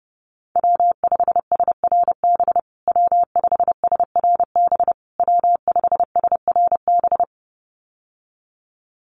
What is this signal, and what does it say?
Morse code, heard as a keyed tone, about 30 wpm, 700 Hz. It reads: W5SRB W5SRB W5SRB
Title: USA Call Signs Repeated 3x 30wpm